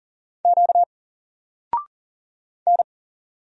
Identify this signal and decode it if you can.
Morse code, heard as a keyed tone, about 40 wpm, 700 Hz. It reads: Q N
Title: Single Letter-Number Alt 40wpm